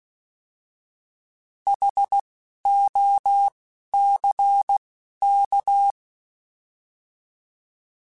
Morse code, heard as a keyed tone, about 16 wpm, 800 Hz. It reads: H O C K